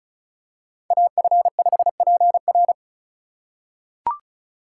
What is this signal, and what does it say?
Morse code, heard as a keyed tone, about 36 wpm, 700 Hz. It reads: AF5PR
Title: USA Call Signs - 5 Characters - 2x Character Spacing - 35wpm